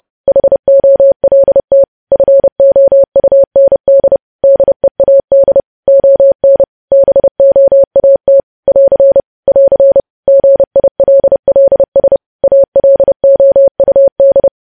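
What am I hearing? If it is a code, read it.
Morse code, heard as a keyed tone, about 30 wpm, 550 Hz. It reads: HOLT FOUND DEAD ON BOAT <AR> <AR> GILLS ALOUD
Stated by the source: News headlines sent in Morse at 30 WPM.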